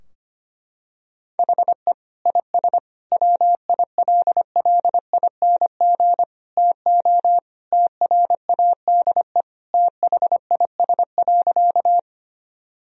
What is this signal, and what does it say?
Morse code, heard as a keyed tone, about 25 wpm, 700 Hz. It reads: HE IS WILLING TO TRADE THIS.
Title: Sentences from Top 1000 Words - Review Part 4 - 25wpm